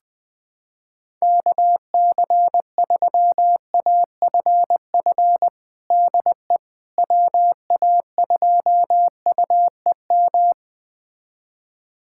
Morse code, heard as a keyed tone, about 20 wpm, 700 Hz. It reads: KC3AFF DE WA2UEM
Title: USA Callsign DE Callsign - 6 Characters - 20wpm